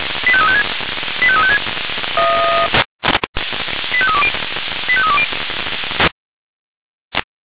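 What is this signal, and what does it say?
Five-tone selective call (ZVEI), heard as quick runs of five tones, tones of about 70 ms. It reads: 85356 85356 85320 85320
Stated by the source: ZVEI five-tone selective call, received by SDR.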